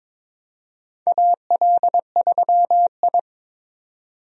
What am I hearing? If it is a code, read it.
Morse code, heard as a keyed tone, about 22 wpm, 700 Hz. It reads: AL3I
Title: USA Call Signs - 4 Characters - 22wpm